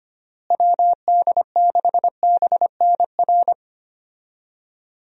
Morse code, heard as a keyed tone, about 25 wpm, 700 Hz. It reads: WD6BNR
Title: USA Call Signs - 6 Characters - Repeated 2x - 25wpm